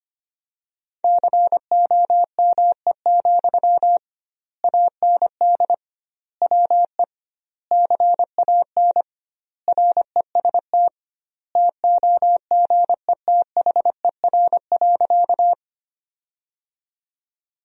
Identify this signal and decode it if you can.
Morse code, heard as a keyed tone, about 25 wpm, 700 Hz. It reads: COME, AND WE CAN REST TOGETHER.
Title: 2x Word-Spacing - Sentences from Top 500 Words - 25wpm